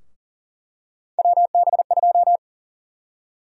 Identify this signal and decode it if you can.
Morse code, heard as a keyed tone, about 40 wpm, 700 Hz. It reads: WB2
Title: USA Call Sign Prefixes - 40wpm